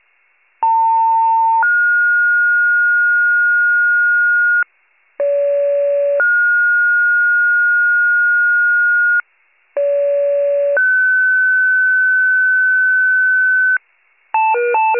Two-tone page, heard as a one-second tone, then a three-second tone, about 900 then 1430 Hz, about 570 then 1430 Hz, about 570 then 1500 Hz. It is heard from 0.6 to 13.8 s.